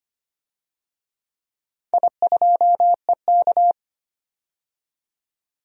Morse code, heard as a keyed tone, about 25 wpm, 700 Hz. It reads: I2EK
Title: DXCC-100 Call Signs - 4 Characters - 25wpm